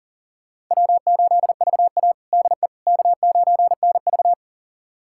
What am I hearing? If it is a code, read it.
Morse code, heard as a keyed tone, about 40 wpm, 700 Hz. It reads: W8VA DE K9NV
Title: USA Callsign DE Callsign - 4 Characters - 40wpm